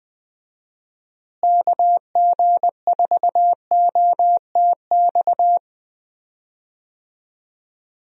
Morse code, heard as a keyed tone, about 20 wpm, 700 Hz. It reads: KG4OTX
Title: USA Call Signs - 6 Characters - Repeated 3x - 20wpm